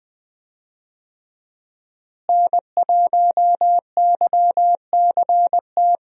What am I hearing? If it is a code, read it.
Morse code, heard as a keyed tone, about 20 wpm, 700 Hz. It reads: N1YCT
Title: USA Call Signs - 5 Characters - Repeated 3x - 20wpm